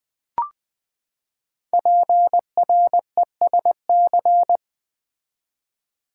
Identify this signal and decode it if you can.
Morse code, heard as a keyed tone, about 20 wpm, 700 Hz. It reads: PRESC
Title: Build A Word - 5 Letters At A Time - 12-Letter Words - 20wpm